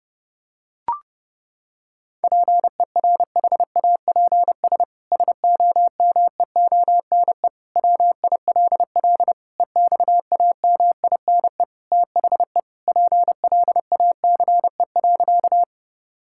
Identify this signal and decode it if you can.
Morse code, heard as a keyed tone, about 30 wpm, 700 Hz. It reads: PERHAPS SOMEONE WILL EXAMINE THE PLACE.
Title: Sentences from Top 1600 Words 30wpm